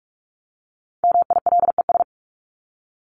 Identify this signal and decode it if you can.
Morse code, heard as a keyed tone, about 44 wpm, 700 Hz. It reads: MILES
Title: ES:S Suffix - Top 100 Words with Base Words 45wpm